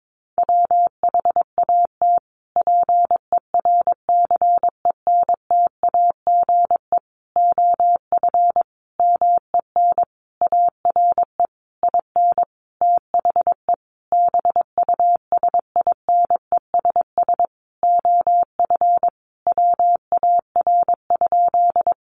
Morse code, heard as a keyed tone, about 22 wpm, 700 Hz. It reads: WHAT PERCENTAGE OF MEN ARE IN THE BUSINESS OF WAR?
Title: Sentences from Top 1300 Words 22wpm